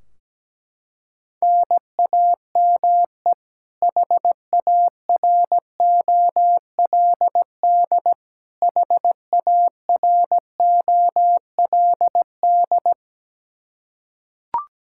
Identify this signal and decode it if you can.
Morse code, heard as a keyed tone, about 17 wpm, 700 Hz. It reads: NAME HAROLD HAROLD
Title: K1USN SST QSO Practice 17wpm